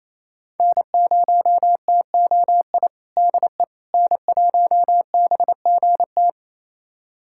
Morse code, heard as a keyed tone, about 28 wpm, 700 Hz. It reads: N0TOI DE N1BGT
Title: USA Callsign DE Callsign - 5 Characters - 28wpm